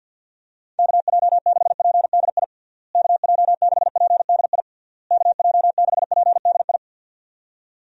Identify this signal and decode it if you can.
Morse code, heard as a keyed tone, about 50 wpm, 700 Hz. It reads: KJ6PDI KJ6PDI KJ6PDI
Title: USA Call Signs - 6 Characters - Repeated 3x - 50wpm